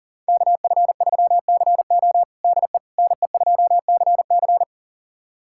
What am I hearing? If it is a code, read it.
Morse code, heard as a keyed tone, about 40 wpm, 700 Hz. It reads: KF3CO DE NE2CC
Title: USA Callsign DE Callsign 40wpm